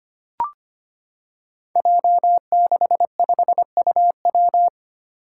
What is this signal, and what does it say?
Morse code, heard as a keyed tone, about 25 wpm, 700 Hz. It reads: J65UW
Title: Five Challenging Letter-Number Combinations 25wpm